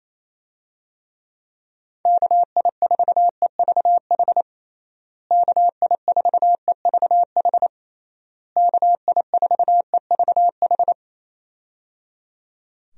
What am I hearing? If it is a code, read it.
Morse code, heard as a keyed tone, about 27 wpm, 700 Hz. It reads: KI4EVH KI4EVH KI4EVH
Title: USA Call Signs - 6 Characters - Repeated 3x - 28wpm